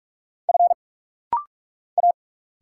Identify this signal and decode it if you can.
Morse code, heard as a keyed tone, about 46 wpm, 700 Hz. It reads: F A
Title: Single Letters - Rapid-Fire 45wpm